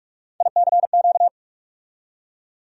Morse code, heard as a keyed tone, about 46 wpm, 700 Hz. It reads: ICQ
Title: USA Call Sign Postfixes - 3 Letters 45wpm